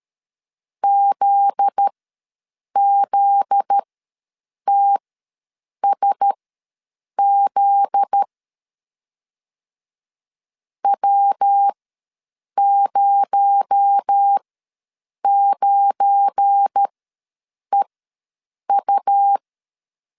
Morse code, heard as a keyed tone, about 13 wpm, spaced as for 7 wpm, 800 Hz. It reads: ZZTSZ W09EU
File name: 7 GF9Z ws 12wpm